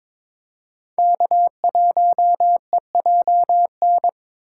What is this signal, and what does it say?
Morse code, heard as a keyed tone, about 22 wpm, 700 Hz. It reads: K1EJN
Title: USA Call Signs - 5 Characters - 6x Character Spacing - 22wpm